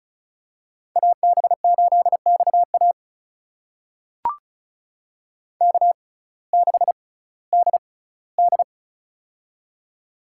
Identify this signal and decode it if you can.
Morse code, heard as a keyed tone, about 36 wpm, spaced as for 19 wpm, 700 Hz. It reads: AB8XA K6DD
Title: USA Call Signs - 6x Character Spacing - 35wpm